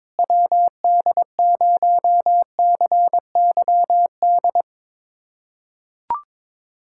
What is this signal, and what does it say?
Morse code, heard as a keyed tone, about 22 wpm, 700 Hz. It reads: WD0CYD
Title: USA Callsign DE Callsign - 6 Characters - 22wpm